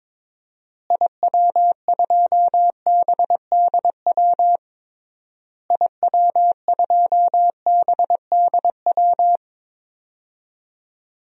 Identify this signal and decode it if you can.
Morse code, heard as a keyed tone, about 22 wpm, 700 Hz. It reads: IW2BDW IW2BDW
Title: DXCC-100 Call Signs - 6 Characters - Repeated 3x - 22wpm